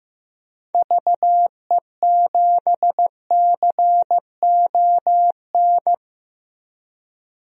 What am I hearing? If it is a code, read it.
Morse code, heard as a keyed tone, about 15 wpm, 700 Hz. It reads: VE7CON
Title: DXCC-100 Call Signs - 6 Characters - Repeated 2x - 15wpm